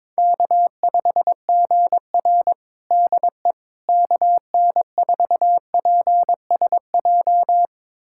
Morse code, heard as a keyed tone, about 22 wpm, 700 Hz. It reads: K5GR DE KN4PSJ
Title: USA Callsign DE Callsign 22wpm